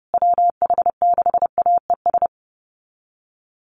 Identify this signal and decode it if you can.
Morse code, heard as a keyed tone, about 28 wpm, 700 Hz. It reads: WH6AES
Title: USA Call Signs Repeated 3x 30wpm